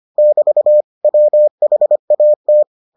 Morse code, heard as a keyed tone, about 25 wpm, 600 Hz. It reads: <BT> WHAT